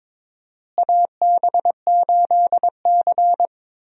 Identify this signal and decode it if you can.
Morse code, heard as a keyed tone, about 22 wpm, 700 Hz. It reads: AB8C
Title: USA Call Signs - 4 Characters - Repeated 2x - 22wpm